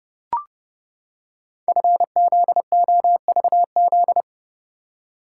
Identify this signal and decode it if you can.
Morse code, heard as a keyed tone, about 29 wpm, 690 Hz. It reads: FZOVZ
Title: Code Groups - 5 Letters - 30wpm